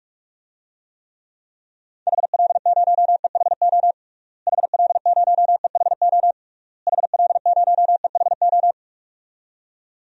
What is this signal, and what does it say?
Morse code, heard as a keyed tone, about 46 wpm, 700 Hz. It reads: HL0EHO HL0EHO HL0EHO
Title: DXCC-100 Call Signs - 6 Characters - Repeated 3x - 45wpm